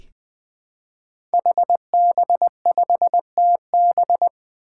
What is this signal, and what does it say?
Morse code, heard as a keyed tone, about 20 wpm, 700 Hz. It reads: HB5TB
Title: Five Challenging Letter-Number Combinations 20wpm